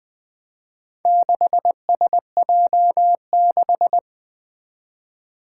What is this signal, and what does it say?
Morse code, heard as a keyed tone, about 20 wpm, 700 Hz. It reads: 6SJ6
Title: Four Challenging Letter-Number Combinations 20wpm